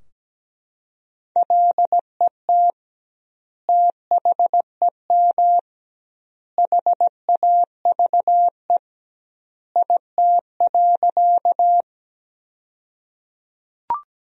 Morse code, heard as a keyed tone, about 17 wpm, 700 Hz. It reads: LET THEM HAVE IT.
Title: 2x Word-Spacing - Sentences from Top 300 Words - 17wpm